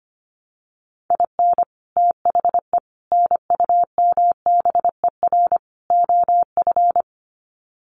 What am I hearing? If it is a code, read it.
Morse code, heard as a keyed tone, about 25 wpm, 700 Hz. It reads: IN THE NUMBER OF
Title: Sets of 4 Words 25wpm